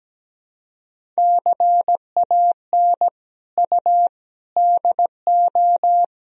Morse code, heard as a keyed tone, about 17 wpm, 700 Hz. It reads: CAN U DO
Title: Sentences from Top 100 Words and Abbreviations 17wpm